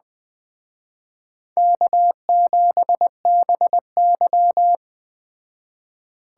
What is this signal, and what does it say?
Morse code, heard as a keyed tone, about 21 wpm, 700 Hz. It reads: K7BY